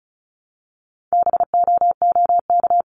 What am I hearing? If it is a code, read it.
Morse code, heard as a keyed tone, about 34 wpm, 700 Hz. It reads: BOOK
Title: Top 500 Words 35wpm